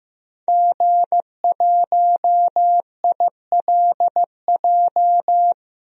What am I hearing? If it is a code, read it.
Morse code, heard as a keyed tone, about 15 wpm, 700 Hz. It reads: G1ILJ
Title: DXCC-100 Call Signs - 5 Characters - Repeated 2x - 15wpm